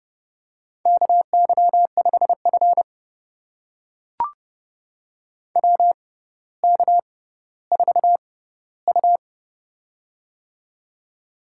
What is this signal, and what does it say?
Morse code, heard as a keyed tone, about 31 wpm, spaced as for 16 wpm, 700 Hz. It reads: KY5F WK4U
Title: USA Call Signs - 4 Characters - 6x Character Spacing - 30wpm